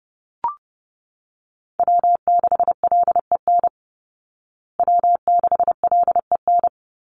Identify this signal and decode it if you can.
Morse code, heard as a keyed tone, about 29 wpm, 700 Hz. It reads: W6LEN W6LEN
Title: Call Signs Repeated 3x - 30wpm